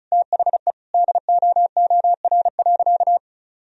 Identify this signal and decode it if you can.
Morse code, heard as a keyed tone, about 36 wpm, 700 Hz. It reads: THE DOOR.
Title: Sentences from Top 400 Words and Abbreviations 35wpm